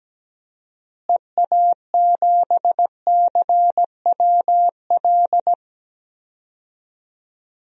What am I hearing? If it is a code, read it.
Morse code, heard as a keyed tone, about 17 wpm, 700 Hz. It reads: EA7CWL
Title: DXCC-100 Call Signs - 6 Characters - Repeated 3x - 17wpm